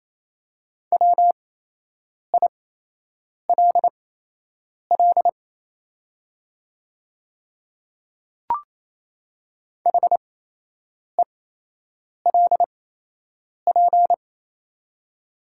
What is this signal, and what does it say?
Morse code, heard as a keyed tone, about 28 wpm, spaced as for 8 wpm, 700 Hz. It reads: WILL HELP
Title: 8x Character Spacing - Top 200 Words - 28wpm